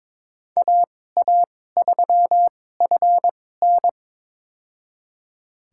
Morse code, heard as a keyed tone, about 22 wpm, spaced as for 16 wpm, 700 Hz. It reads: AA3FN
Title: USA Call Signs - 5 Characters - 2x Character Spacing - 22wpm